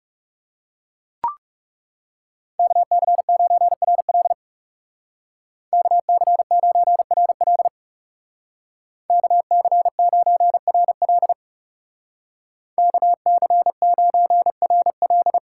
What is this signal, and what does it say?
Morse code, heard as a keyed tone, about 36 wpm, 700 Hz. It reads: KC9RL KC9RL KC9RL KC9RL
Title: Speed Racing - USA 2x2 Call Signs 30wpm